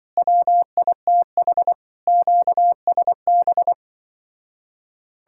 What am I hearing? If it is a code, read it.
Morse code, heard as a keyed tone, about 24 wpm, 700 Hz. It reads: WITH QSB
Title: Speed-Racing Common QSO Phrases - 20wpm